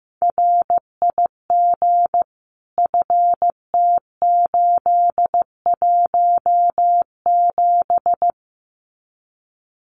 Morse code, heard as a keyed tone, about 15 wpm, 700 Hz. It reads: RIG FT817
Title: Common QSO Phrases Encore 15wpm